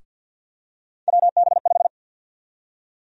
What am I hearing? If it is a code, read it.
Morse code, heard as a keyed tone, about 51 wpm, 700 Hz. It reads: WB5
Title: USA Call Sign Prefixes - Repeated 2x - 50wpm